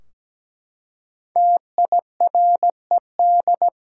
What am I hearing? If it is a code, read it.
Morse code, heard as a keyed tone, about 17 wpm, 700 Hz. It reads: TIRED
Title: Five Letter Words - 17wpm